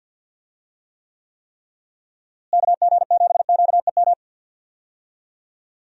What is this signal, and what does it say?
Morse code, heard as a keyed tone, about 49 wpm, 700 Hz. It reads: KG7QEM
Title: USA Call Signs - 6 Characters - Repeated 3x - 50wpm